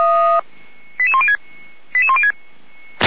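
Five-tone selective call (ZVEI), heard as quick runs of five tones, tones of about 70 ms. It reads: 88106 88106
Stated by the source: ZVEI five-tone selective call 98811 received via SDR.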